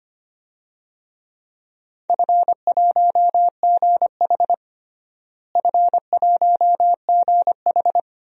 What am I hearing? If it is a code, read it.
Morse code, heard as a keyed tone, about 25 wpm, 700 Hz. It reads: F1GH F1GH
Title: DXCC-100 Call Signs - 4 Characters - Repeated 2x - 25wpm